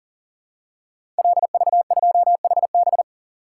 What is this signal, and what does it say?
Morse code, heard as a keyed tone, about 41 wpm, 700 Hz. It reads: LV2HB
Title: Code Groups - 5 Letter:Numbers - 40wpm